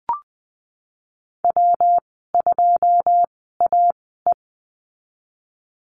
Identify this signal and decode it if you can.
Morse code, heard as a keyed tone, about 20 wpm, spaced as for 14 wpm, 700 Hz. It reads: W2AE
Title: 154 Lesson 25g - Letter p - Call Signs 20wpm